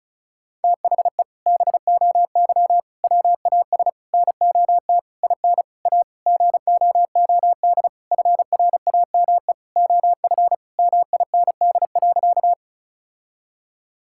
Morse code, heard as a keyed tone, about 36 wpm, 700 Hz. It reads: THE BOY WAS NOT IN A GOOD FRAME OF MIND.
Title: Sentences from Top 1000 Words - Review Part 6 - 35wpm